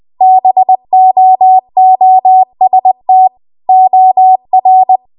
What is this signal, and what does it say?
Morse code, heard as a keyed tone, about 20 wpm, 750 Hz. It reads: BOOST OR